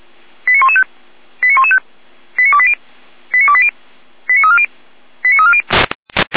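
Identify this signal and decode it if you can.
Five-tone selective call (ZVEI), heard as quick runs of five tones, tones of about 70 ms. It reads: 89115 89115 89280 89280 89340 89340
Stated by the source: ZVEI five-tone selective call received on SDR, code 89020.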